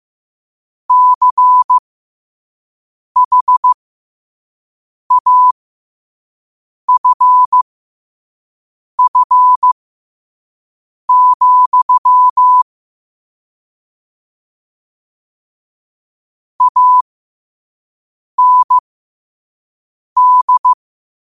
Morse code, heard as a keyed tone, about 15 wpm, spaced as for 5 wpm, 1000 Hz. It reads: CHAFF, AND
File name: qotdfiles / 5wpm